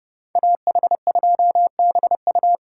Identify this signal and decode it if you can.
Morse code, heard as a keyed tone, about 29 wpm, 700 Hz. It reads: AH2BU